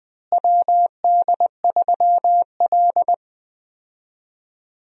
Morse code, heard as a keyed tone, about 20 wpm, 700 Hz. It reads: WD3L
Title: USA Call Signs - 4 Characters - Repeated 2x - 20wpm